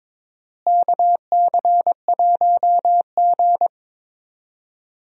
Morse code, heard as a keyed tone, about 22 wpm, 700 Hz. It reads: KC1G
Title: USA Call Signs - 4 Characters - Repeated 2x - 22wpm